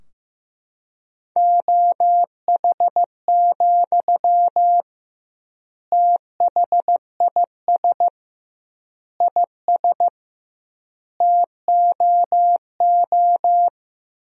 Morse code, heard as a keyed tone, about 15 wpm, 700 Hz. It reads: OH, THIS IS TOO
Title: 2x Word-Spacing - Sentences from Top 500 Words - 15wpm